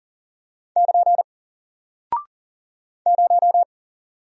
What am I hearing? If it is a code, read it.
Morse code, heard as a keyed tone, about 39 wpm, 700 Hz. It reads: <KN> 0
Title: Single Character (Letter:Number:Punctuation:Prosign) Alt 40wpm